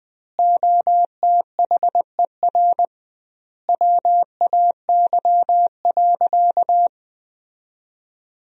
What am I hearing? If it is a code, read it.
Morse code, heard as a keyed tone, about 20 wpm, 700 Hz. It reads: OTHER WAY.
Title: 2x Word-Spacing - Sentences from Top 100 Words - 20wpm